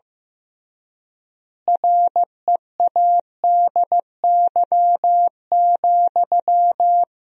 Morse code, heard as a keyed tone, about 15 wpm, 700 Hz. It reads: READY,
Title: 3x Word-Spacing - Sentences from Top 500 Words - 15wpm